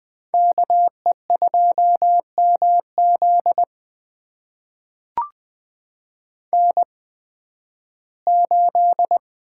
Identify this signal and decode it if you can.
Morse code, heard as a keyed tone, about 20 wpm, spaced as for 13 wpm, 700 Hz. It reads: KE2MZ N8
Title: USA Call Signs - 5 Characters - 8x Character Spacing - 20wpm